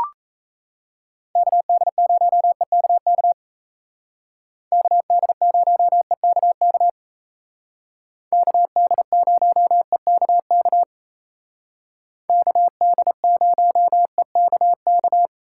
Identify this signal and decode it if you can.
Morse code, heard as a keyed tone, about 36 wpm, 700 Hz. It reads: KD0EKK KD0EKK KD0EKK KD0EKK